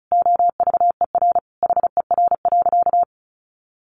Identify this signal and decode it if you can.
Morse code, heard as a keyed tone, about 34 wpm, 700 Hz. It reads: OVER HER.
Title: Sentences from Top 1200 Words 35wpm